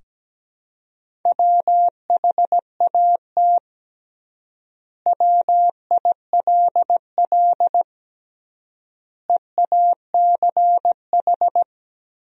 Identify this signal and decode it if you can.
Morse code, heard as a keyed tone, about 17 wpm, 700 Hz. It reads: WHAT WILL EACH
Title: 3x Word-Spacing - Sentences from Top 200 Words - 17wpm